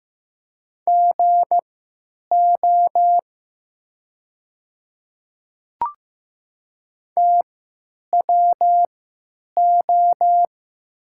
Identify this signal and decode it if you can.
Morse code, heard as a keyed tone, about 15 wpm, spaced as for 9 wpm, 700 Hz. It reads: GO TWO
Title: 3x Character Spacing - Top 200 Words - 15wpm